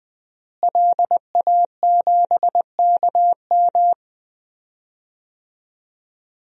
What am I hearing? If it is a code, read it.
Morse code, heard as a keyed tone, about 20 wpm, 700 Hz. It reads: LA7KM
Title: DXCC-100 Call Signs - 5 Characters - Repeated 2x - 20wpm